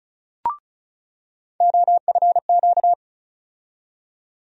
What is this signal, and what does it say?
Morse code, heard as a keyed tone, about 36 wpm, 700 Hz. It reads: OFQ
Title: USA Call Sign Postfixes - 3 Letters 35wpm